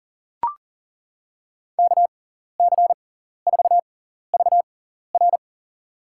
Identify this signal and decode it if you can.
Morse code, heard as a keyed tone, about 40 wpm, spaced as for 14 wpm, 700 Hz. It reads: KC4VR
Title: USA Call Signs - 5 Characters - 6x Character Spacing - 40wpm